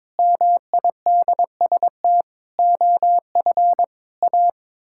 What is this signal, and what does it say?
Morse code, heard as a keyed tone, about 21 wpm, 700 Hz. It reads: MIDST OF A
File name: practice / 22wpm / Sets of 7 Words - 22wpm